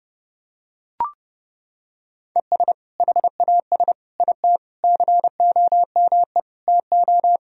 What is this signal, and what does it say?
Morse code, heard as a keyed tone, about 30 wpm, 700 Hz. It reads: ES HAS IT COME TO
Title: Sentences from Top 100 Words and Abbreviations 30wpm